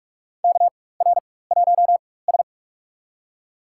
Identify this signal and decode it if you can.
Morse code, heard as a keyed tone, about 46 wpm, spaced as for 21 wpm, 700 Hz. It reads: KR1S
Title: USA Call Signs - 4 Characters - 4x Character Spacing - 45wpm